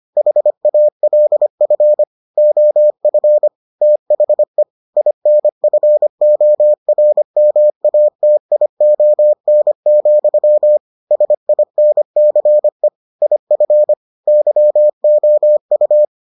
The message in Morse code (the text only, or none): HALF OF THE INFORMATION, SINCE IF YOU